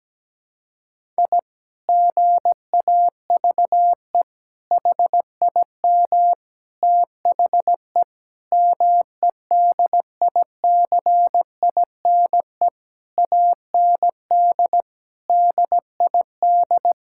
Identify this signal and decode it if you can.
Morse code, heard as a keyed tone, about 17 wpm, 700 Hz. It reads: I GAVE HIM THE MEDICINE AND DID
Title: Sentences from Top 1000 Words - Review Part 6 - 17wpm